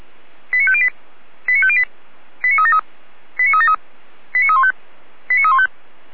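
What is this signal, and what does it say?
Five-tone selective call (ZVEI), heard as quick runs of five tones, tones of about 70 ms. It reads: 89508 89508 89373 89373 89315 89315
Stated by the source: ZVEI selective call received on SDR.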